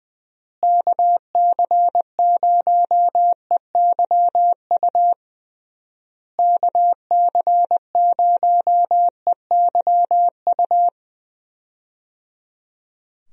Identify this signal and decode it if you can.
Morse code, heard as a keyed tone, about 20 wpm, 700 Hz. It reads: KC0EYU KC0EYU
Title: USA Call Signs - 6 Characters - Repeated 3x - 20wpm